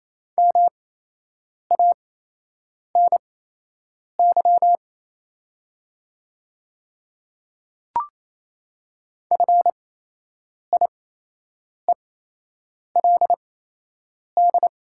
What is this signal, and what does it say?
Morse code, heard as a keyed tone, about 28 wpm, spaced as for 8 wpm, 690 Hz. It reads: MANY FIELD
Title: 8x Character Spacing - Top 500 Words - 28wpm